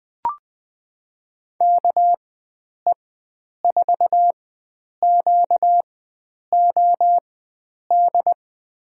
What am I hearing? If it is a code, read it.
Morse code, heard as a keyed tone, about 20 wpm, spaced as for 9 wpm, 700 Hz. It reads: KE4QOD
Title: USA Call Signs - 6 Characters - 4x Character Spacing - 20wpm